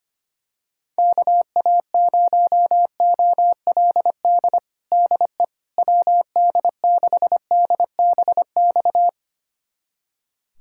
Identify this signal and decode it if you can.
Morse code, heard as a keyed tone, about 25 wpm, 700 Hz. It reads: KA0OLD DE WD6DBX
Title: USA Callsign DE Callsign - 6 Characters - 25wpm